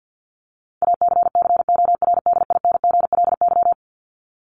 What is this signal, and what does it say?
Morse code, heard as a keyed tone, about 50 wpm, 700 Hz. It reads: ACCORDINGLY
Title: LY Suffix - Top 100 Words with Base Words 50wpm